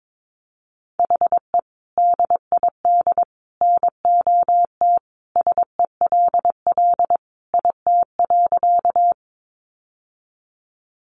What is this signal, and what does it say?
Morse code, heard as a keyed tone, about 22 wpm, 700 Hz. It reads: HE DID NOT SELL IT.
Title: Sentences from Top 1100 Words 22wpm